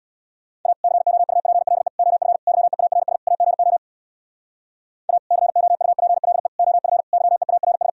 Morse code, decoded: A COUPLE OF YEARS AGO A COUPLE OF YEARS

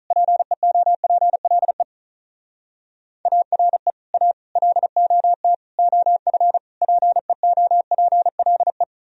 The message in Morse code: PEOPLE ARE A LOT OF PEOPLE

